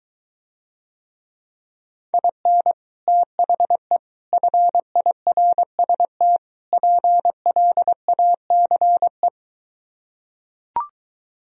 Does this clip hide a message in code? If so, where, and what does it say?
Yes: Morse code from 2.1 to 9.3 s, heard as a keyed tone, about 24 wpm, 700 Hz. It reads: IN THE FIRST PLACE